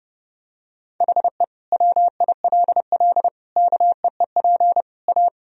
Morse code, heard as a keyed tone, about 32 wpm, 700 Hz. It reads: HE WILL KEEP A
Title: Sentences from Top 1000 Words - Review Part 3 - 30wpm